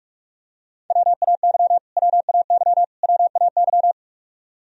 Morse code, heard as a keyed tone, about 43 wpm, 700 Hz. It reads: WAY WAY WAY
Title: Top 100 Words Repeated 45wpm